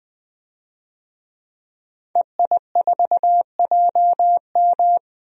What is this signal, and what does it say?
Morse code, heard as a keyed tone, about 20 wpm, 700 Hz. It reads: EI4JM
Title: DXCC-100 Call Signs - 5 Characters - 20wpm